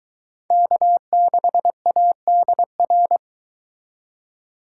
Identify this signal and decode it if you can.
Morse code, heard as a keyed tone, about 22 wpm, 700 Hz. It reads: K6ADR